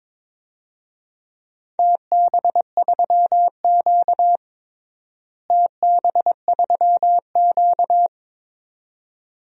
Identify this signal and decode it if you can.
Morse code, heard as a keyed tone, about 22 wpm, 700 Hz. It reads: TB3Q TB3Q
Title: DXCC-100 Call Signs - 4 Characters - Repeated 2x - 22wpm